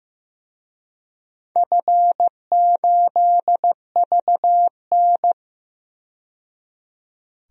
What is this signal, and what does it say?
Morse code, heard as a keyed tone, about 15 wpm, 700 Hz. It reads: F8VN
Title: DXCC-100 Call Signs - 4 Characters - 15wpm